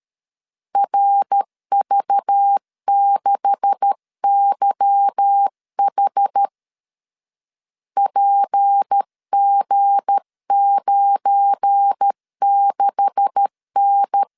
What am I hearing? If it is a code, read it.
Morse code, heard as a keyed tone, about 13 wpm, 800 Hz. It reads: RV6YH PG96N
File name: LEZIONI / 9 286 12wpm 2305